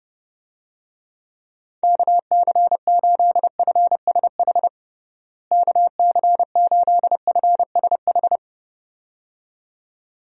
Morse code, heard as a keyed tone, about 30 wpm, 700 Hz. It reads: KC8FSH KC8FSH
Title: USA Call Signs - 6 Characters - Repeated 2x - 30wpm